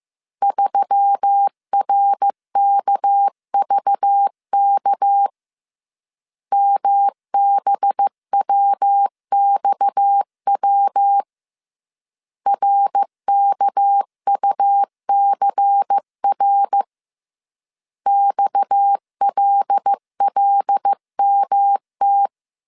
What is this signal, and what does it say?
Morse code, heard as a keyed tone, about 15 wpm, 800 Hz. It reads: 3RKVK MBWXW RKUCR XLLMT
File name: CW / DET / 1 6 1500C S A 14wpm 2921